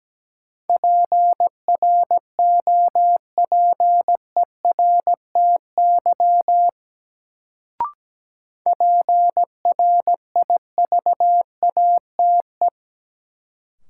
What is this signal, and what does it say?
Morse code, heard as a keyed tone, about 17 wpm, 700 Hz. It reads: PROPERTY PRIVATE
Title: Sentences from Top 1000 Words - New Word Drill - 17wpm